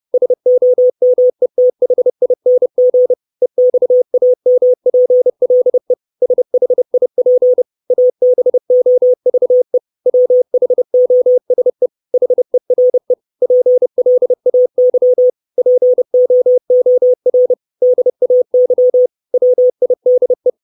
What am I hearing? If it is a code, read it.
Morse code, heard as a keyed tone, about 30 wpm, 500 Hz. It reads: SOMETHING EXAMPLE SHIP ABOVE WHOSE HERE PLAY POOR DAY WIDE
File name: 632 Words 30WPM